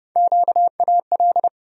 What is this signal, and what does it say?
Morse code, heard as a keyed tone, about 29 wpm, 700 Hz. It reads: QAL
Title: USA Call Sign Postfixes - 3 Letters 30wpm